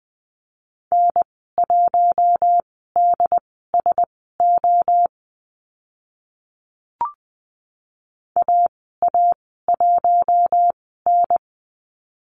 Morse code, heard as a keyed tone, about 20 wpm, spaced as for 14 wpm, 700 Hz. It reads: N1DSO AA1N
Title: 069 Lesson 12g - Letter d - Call Signs 20wpm